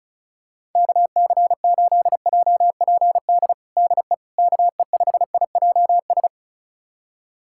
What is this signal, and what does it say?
Morse code, heard as a keyed tone, about 36 wpm, 700 Hz. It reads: KC8JPD DE KE5IJS
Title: USA Callsign DE Callsign - 6 Characters - 35wpm